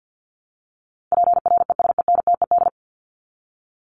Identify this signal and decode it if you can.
Morse code, heard as a keyed tone, about 50 wpm, 700 Hz. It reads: PRESENTED
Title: ED Suffix - Top 100 Words with Base Words 50wpm